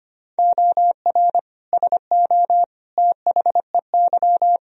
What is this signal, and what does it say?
Morse code, heard as a keyed tone, about 25 wpm, 700 Hz. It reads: OR SO THEY
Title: Sentences from Top 100 Words and Abbreviations 25wpm